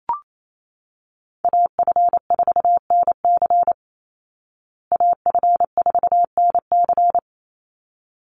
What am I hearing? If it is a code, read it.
Morse code, heard as a keyed tone, about 28 wpm, 700 Hz. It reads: AF4NC AF4NC
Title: Call Signs Repeated 3x - 28wpm